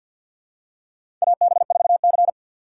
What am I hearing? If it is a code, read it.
Morse code, heard as a keyed tone, about 48 wpm, 700 Hz. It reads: AB4C
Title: USA Call Signs - 4 Characters - 6x Character Spacing - 50wpm